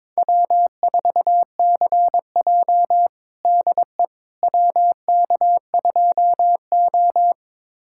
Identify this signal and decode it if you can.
Morse code, heard as a keyed tone, about 22 wpm, 700 Hz. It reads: W4CJ DE WK2O
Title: USA Callsign DE Callsign - 4 Characters - 22wpm